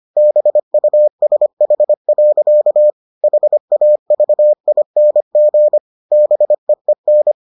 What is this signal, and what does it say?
Morse code, heard as a keyed tone, about 25 wpm, 600 Hz. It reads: BUSH. HAVING BEEN